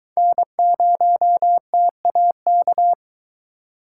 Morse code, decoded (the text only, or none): N0TAK